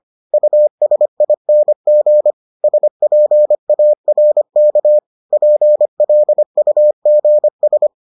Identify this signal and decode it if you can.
Morse code, heard as a keyed tone, about 25 wpm, 600 Hz. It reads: USING SPARK PLUGS